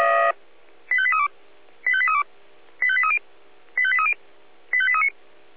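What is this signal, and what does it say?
Five-tone selective call (ZVEI), heard as quick runs of five tones, tones of about 70 ms. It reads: 85832 85832 85830 85830 85839
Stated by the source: ZVEI five-tone selective call received by SDR.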